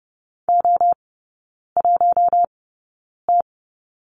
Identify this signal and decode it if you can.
Morse code, heard as a keyed tone, about 30 wpm, spaced as for 9 wpm, 700 Hz. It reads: O1T
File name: Lesson 16e - Letter c - Three Characters 30wpm